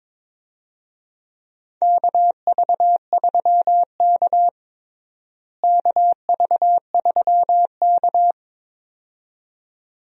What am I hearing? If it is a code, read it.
Morse code, heard as a keyed tone, about 22 wpm, 700 Hz. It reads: KV3K KV3K
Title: USA Call Signs - 4 Characters - Repeated 2x - 22wpm